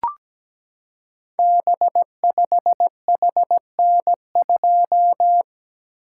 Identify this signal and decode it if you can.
Morse code, heard as a keyed tone, about 17 wpm, 700 Hz. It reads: B5HN2
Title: Code Groups - 5 Letter:Numbers - 17wpm